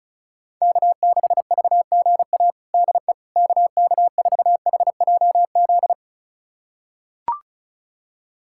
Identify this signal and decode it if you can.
Morse code, heard as a keyed tone, about 36 wpm, 700 Hz. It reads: K6VGA DE KK4HJZ
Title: USA Callsign DE Callsign 35wpm